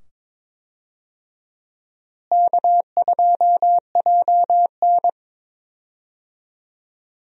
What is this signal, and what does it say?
Morse code, heard as a keyed tone, about 22 wpm, 700 Hz. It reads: K2JN
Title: USA Call Signs - 4 Characters - Repeated 2x - 22wpm